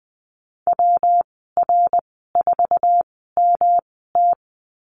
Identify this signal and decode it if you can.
Morse code, heard as a keyed tone, about 20 wpm, spaced as for 15 wpm, 700 Hz. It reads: WR4MT
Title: 167 Lesson 27g - Number 7 - Call Signs 20wpm